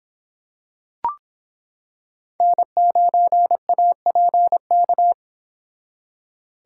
Morse code, heard as a keyed tone, about 26 wpm, 700 Hz. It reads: N9APK